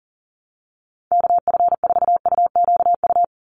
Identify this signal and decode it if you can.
Morse code, heard as a keyed tone, about 39 wpm, 700 Hz. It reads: KF4UQU
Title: USA Call Signs 40wpm